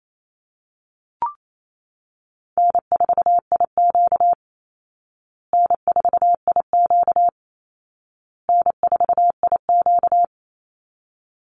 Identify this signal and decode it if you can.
Morse code, heard as a keyed tone, about 28 wpm, 700 Hz. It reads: N4IQ N4IQ N4IQ
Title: Call Signs Repeated 2x - 28wpm